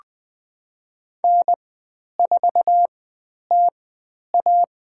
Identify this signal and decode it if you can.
Morse code, heard as a keyed tone, about 20 wpm, spaced as for 10 wpm, 700 Hz. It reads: N4TA
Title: 095 Lesson 16g - Letter c - Call Signs 20wpm